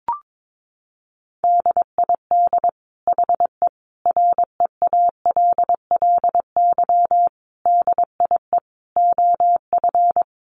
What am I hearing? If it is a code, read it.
Morse code, heard as a keyed tone, about 22 wpm, 700 Hz. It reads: DID HE REALLY DIE OF
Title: Sentences from Top 1300 Words 22wpm